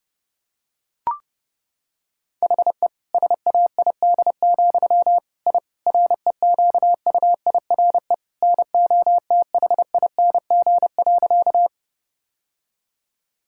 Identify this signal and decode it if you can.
Morse code, heard as a keyed tone, about 30 wpm, 700 Hz. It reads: HE SAID, I REQUIRE NOTHING.
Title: Sentences from Top 1000 Words - Review Part 6 - 30wpm